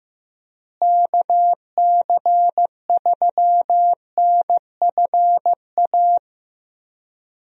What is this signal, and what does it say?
Morse code, heard as a keyed tone, about 15 wpm, 700 Hz. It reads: KC3NFA
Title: USA Call Signs - 6 Characters - Repeated 2x - 15wpm